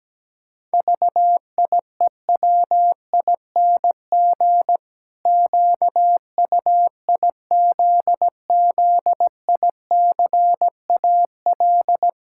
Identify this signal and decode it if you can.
Morse code, heard as a keyed tone, about 17 wpm, 700 Hz. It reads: VIEWING QUIZZICAL
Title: Pangram Challenge v2 - 17wpm